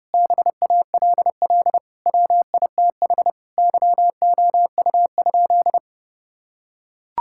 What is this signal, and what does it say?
Morse code, heard as a keyed tone, about 32 wpm, 700 Hz. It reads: BALL WITH YOU?
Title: Sentences from Top 1000 Words - Review Part 6 - 30wpm